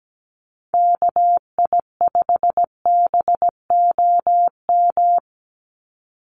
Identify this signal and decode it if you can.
Morse code, heard as a keyed tone, about 17 wpm, 700 Hz. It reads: KI5BOM
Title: USA Call Signs 17wpm